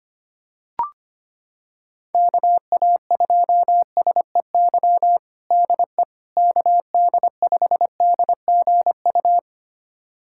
Morse code, heard as a keyed tone, about 25 wpm, 700 Hz. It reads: KA2SEY DE KD5DGU
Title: USA Callsign DE Callsign 25wpm